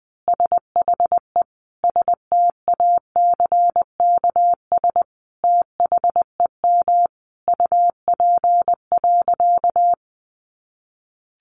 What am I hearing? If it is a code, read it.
Morse code, heard as a keyed tone, about 20 wpm, 700 Hz. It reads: SHE STACKS THEM UP.
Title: Easy Phrases 20wpm